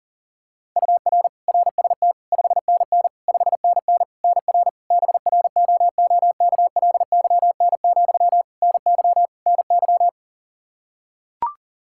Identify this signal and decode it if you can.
Morse code, heard as a keyed tone, about 37 wpm, 700 Hz. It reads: UR RST 5NN 5NN NR BROOKLYN, NY NY
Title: K1USN SST QSO Practice 40wpm